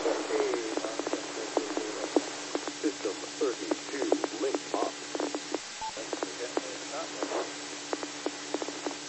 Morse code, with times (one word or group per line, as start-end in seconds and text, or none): none